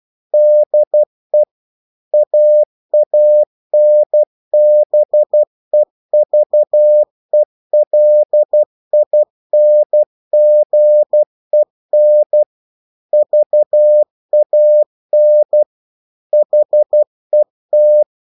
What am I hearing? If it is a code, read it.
Morse code, heard as a keyed tone, about 12 wpm, 600 Hz. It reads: DE AANBEVELINGEN VAN HET